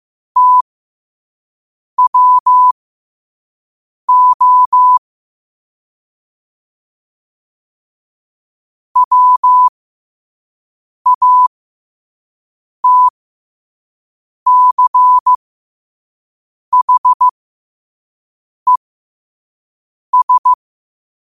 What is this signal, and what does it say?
Morse code, heard as a keyed tone, about 15 wpm, spaced as for 5 wpm, 1000 Hz. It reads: TWO WATCHES
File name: qotdfiles / 5wpm